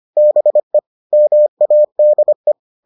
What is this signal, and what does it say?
Morse code, heard as a keyed tone, about 25 wpm, 600 Hz. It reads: BE MADE